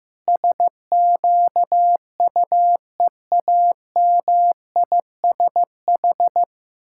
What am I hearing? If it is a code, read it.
Morse code, heard as a keyed tone, about 15 wpm, 700 Hz. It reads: SQUEAMISH
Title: Pangram Challenge v2 - 15wpm